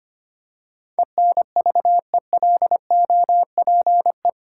Morse code, heard as a keyed tone, about 25 wpm, 700 Hz. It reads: ENVELOPE
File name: Eight Letter Words - 25wpm